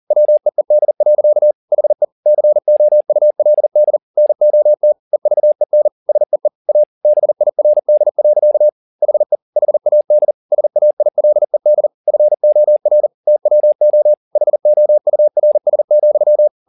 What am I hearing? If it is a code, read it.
Morse code, heard as a keyed tone, about 40 wpm, 600 Hz. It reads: WEED. HE COULD NOT EVEN SEE A BIRD. HE HAD SAILED FOR TWO HOURS,
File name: the old man and the sea 2 40wpm